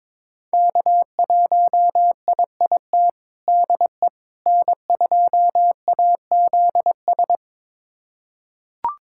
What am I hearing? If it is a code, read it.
Morse code, heard as a keyed tone, about 22 wpm, 700 Hz. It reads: K1IIT DE N2AZS
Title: USA Callsign DE Callsign - 5 Characters - 22wpm